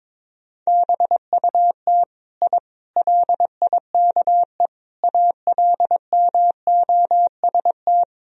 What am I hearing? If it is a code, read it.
Morse code, heard as a keyed tone, about 22 wpm, 700 Hz. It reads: BUT I LIKE ALMOST
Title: Sentences from Top 1000 Words - Review Part 4 - 22wpm